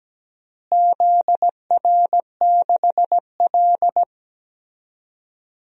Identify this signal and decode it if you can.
Morse code, heard as a keyed tone, about 17 wpm, 700 Hz. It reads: ZR6L
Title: DXCC-100 Call Signs - 4 Characters - Repeated 2x - 17wpm